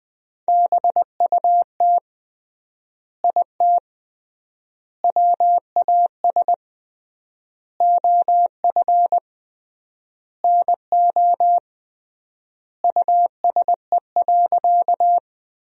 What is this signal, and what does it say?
Morse code, heard as a keyed tone, about 21 wpm, 700 Hz. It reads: BUT IT WAS OF NO USE.
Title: 3x Word-Spacing - Sentences from Top 100 Words - 20wpm